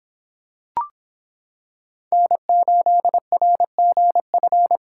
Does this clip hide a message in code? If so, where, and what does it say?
Yes: Morse code from 2.1 to 4.8 s, heard as a keyed tone, about 26 wpm, 700 Hz. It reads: N8RGF